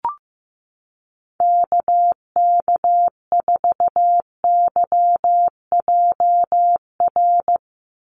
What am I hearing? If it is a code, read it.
Morse code, heard as a keyed tone, about 16 wpm, 700 Hz. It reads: KK4YJR
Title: USA Call Signs 15wpm